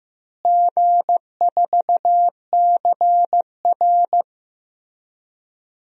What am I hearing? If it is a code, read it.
Morse code, heard as a keyed tone, about 16 wpm, 700 Hz. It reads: G4CR
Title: DXCC-100 Call Signs - 4 Characters - Repeated 3x - 15wpm